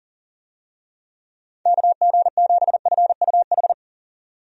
Morse code, heard as a keyed tone, about 40 wpm, 700 Hz. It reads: KG7FUH
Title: USA Call Signs - 6 Characters - 40wpm